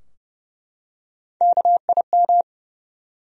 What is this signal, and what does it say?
Morse code, heard as a keyed tone, about 31 wpm, 700 Hz. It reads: KIM
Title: USA Common Names - Max 4 Letters - 30wpm